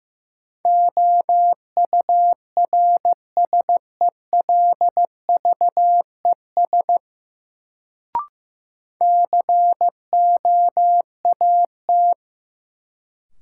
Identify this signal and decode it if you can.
Morse code, heard as a keyed tone, about 15 wpm, 700 Hz. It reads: OURSELVES COAT
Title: Sentences from Top 1600 Words - New Word Drill - 15wpm